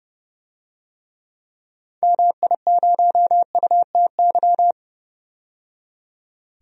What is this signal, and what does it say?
Morse code, heard as a keyed tone, about 32 wpm, 700 Hz. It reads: MI0UTY
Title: DXCC-100 Call Signs - 6 Characters - 30wpm